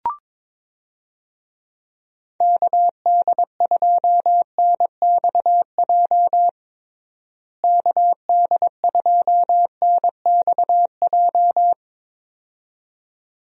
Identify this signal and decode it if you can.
Morse code, heard as a keyed tone, about 22 wpm, 700 Hz. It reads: KD2NXJ KD2NXJ
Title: USA Call Signs - 6 Characters - Repeated 2x - 22wpm